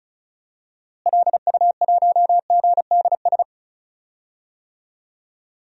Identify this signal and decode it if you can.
Morse code, heard as a keyed tone, about 36 wpm, 700 Hz. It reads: LU1GDS
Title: DXCC-100 Call Signs - 6 Characters - Repeated 3x - 35wpm